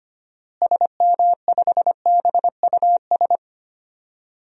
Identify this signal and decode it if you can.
Morse code, heard as a keyed tone, about 25 wpm, 700 Hz. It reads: SM5BUS
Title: DXCC-100 Call Signs - 6 Characters - Repeated 2x - 25wpm